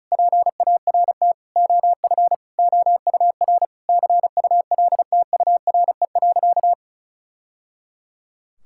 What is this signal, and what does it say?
Morse code, heard as a keyed tone, about 36 wpm, 700 Hz. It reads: PART OF OUR CULTURE.
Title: Sentences from Top 1000 Words - Review Part 6 - 35wpm